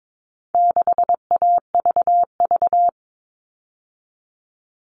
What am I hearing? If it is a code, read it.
Morse code, heard as a keyed tone, about 22 wpm, 700 Hz. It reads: 6AVV
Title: All Cut Numbers - 4-Digit 22wpm